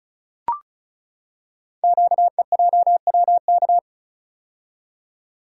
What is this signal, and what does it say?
Morse code, heard as a keyed tone, about 36 wpm, 700 Hz. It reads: QEJWK
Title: Code Groups - 5 Letters - 35wpm